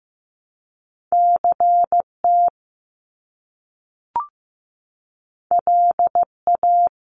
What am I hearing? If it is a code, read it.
Morse code, heard as a keyed tone, about 15 wpm, 700 Hz. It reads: CT LA
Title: US States 15wpm